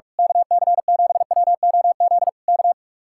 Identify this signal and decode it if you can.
Morse code, heard as a keyed tone, about 44 wpm, 700 Hz. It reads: KC7WOZ K
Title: Speed-Racing Common QSO Phrases - 45wpm